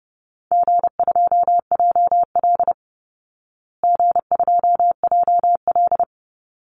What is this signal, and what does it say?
Morse code, heard as a keyed tone, about 30 wpm, 700 Hz. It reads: G2JL G2JL
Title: Easy Call Signs Repeated 2x - 30wpm